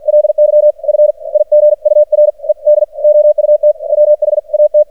Morse code, heard as a keyed tone, about 52 wpm, spaced as for 41 wpm, 600 Hz, with radio noise. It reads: LY FIGURED OUT WHAT
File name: archived / CRSnet